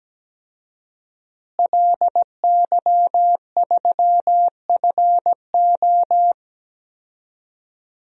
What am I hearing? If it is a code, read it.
Morse code, heard as a keyed tone, about 17 wpm, 700 Hz. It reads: LY3FO
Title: DXCC-100 Call Signs - 5 Characters - Repeated 2x - 17wpm